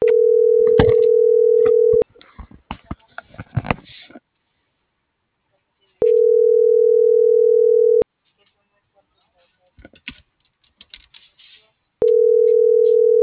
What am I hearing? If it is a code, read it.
Ringback tone (North America), 440 + 480 Hz.